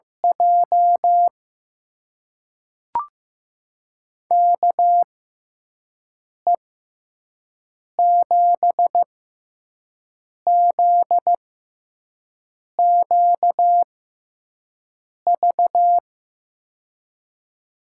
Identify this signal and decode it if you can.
Morse code, heard as a keyed tone, about 15 wpm, spaced as for 5 wpm, 700 Hz. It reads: J KE7ZQV
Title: USA Call Signs - 6x Character Spacing - 15wpm